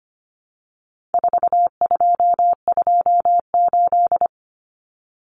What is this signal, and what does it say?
Morse code, heard as a keyed tone, about 25 wpm, 700 Hz. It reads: 4228
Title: Numbers - 4-Digit 25wpm